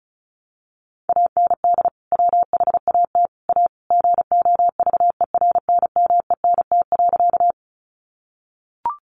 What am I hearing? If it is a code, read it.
Morse code, heard as a keyed tone, about 34 wpm, 700 Hz. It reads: AND WHAT A GOVERNMENT.
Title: Sentences from Top 600 Words 35wpm